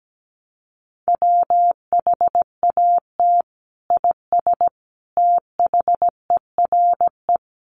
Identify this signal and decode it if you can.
Morse code, heard as a keyed tone, about 17 wpm, 700 Hz. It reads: WHAT IS THERE
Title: Sentences from Top 100 Words 17wpm